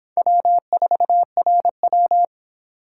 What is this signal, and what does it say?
Morse code, heard as a keyed tone, about 23 wpm, 700 Hz. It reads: W4RW